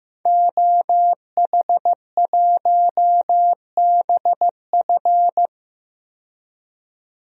Morse code, heard as a keyed tone, about 15 wpm, 700 Hz. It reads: OH1BF
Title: DXCC-100 Call Signs - 5 Characters - 15wpm